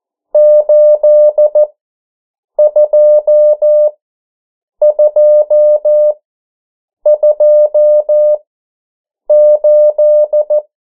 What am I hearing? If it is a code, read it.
Morse code, heard as a keyed tone, about 14 wpm, spaced as for 7 wpm, 600 Hz. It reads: 82228
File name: morse mp3 / 09 urok 8